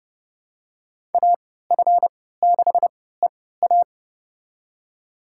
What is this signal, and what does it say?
Morse code, heard as a keyed tone, about 30 wpm, spaced as for 17 wpm, 700 Hz. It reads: AF6EA
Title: USA Call Signs - 5 Characters - 3x Character Spacing - 30wpm